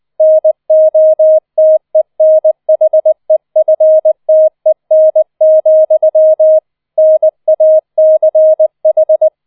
Morse code, heard as a keyed tone, about 20 wpm, 630 Hz. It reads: NOTENHEFTEN, NACH